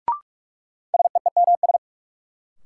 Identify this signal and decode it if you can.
Morse code, heard as a keyed tone, about 44 wpm, 700 Hz. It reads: SEEMS
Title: Sets of 3 Words - New Word Drill - 45wpm